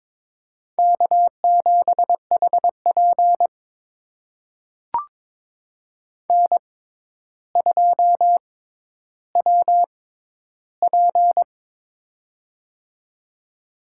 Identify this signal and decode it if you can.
Morse code, heard as a keyed tone, about 21 wpm, spaced as for 11 wpm, 700 Hz. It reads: K7HP N2WP
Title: USA Call Signs - 4 Characters - 6x Character Spacing - 22wpm